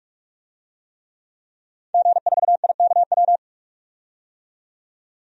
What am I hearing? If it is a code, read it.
Morse code, heard as a keyed tone, about 46 wpm, 700 Hz. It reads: G4IKW
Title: DXCC-100 Call Signs - 5 Characters - 45wpm